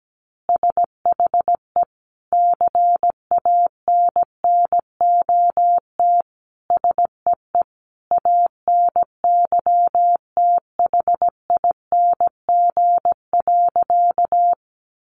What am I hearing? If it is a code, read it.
Morse code, heard as a keyed tone, about 17 wpm, 700 Hz. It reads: SHE CANNOT SEE ANYTHING.
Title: Additional Easy Phrases 17wpm